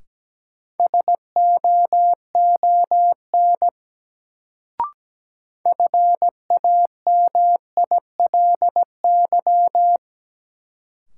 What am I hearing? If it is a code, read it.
Morse code, heard as a keyed tone, about 17 wpm, 700 Hz. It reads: SOON FAMILY
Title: Sentences from Top 400 Words - New Word Drill - 17wpm